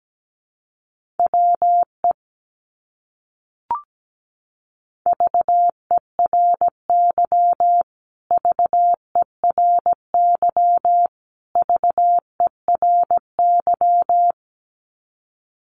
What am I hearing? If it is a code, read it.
Morse code, heard as a keyed tone, about 17 wpm, 700 Hz. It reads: WE VERY VERY VERY
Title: Top 200 Words Repeated 17wpm